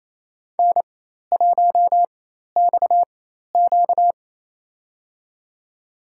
Morse code, heard as a keyed tone, about 28 wpm, spaced as for 13 wpm, 700 Hz. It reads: N1XQ
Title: USA Call Signs - 4x Character Spacing - 28wpm